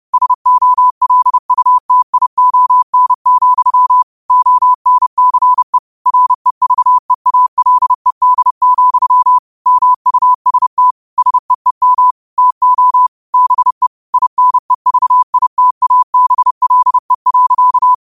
Morse code, heard as a keyed tone, about 30 wpm, 1000 Hz. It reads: SOLUTION, ONCE REVEALED, MUST SEEM TO BE INEVITABLE.